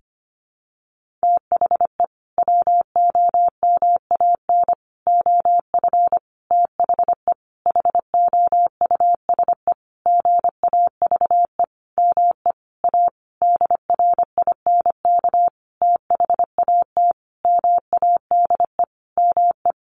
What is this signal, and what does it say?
Morse code, heard as a keyed tone, about 25 wpm, 700 Hz. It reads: THE WOMAN OF THE HOUSE GAVE ME A DRINK THAT MADE ME
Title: Sentences from Top 1500 Words 25wpm